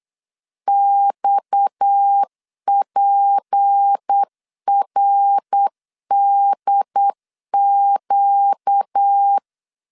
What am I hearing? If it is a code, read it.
Morse code, heard as a keyed tone, about 8 wpm, 800 Hz. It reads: XPRDQ
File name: LEZIONI / 6 PKX0 / 6 PKX0 8wpm 3341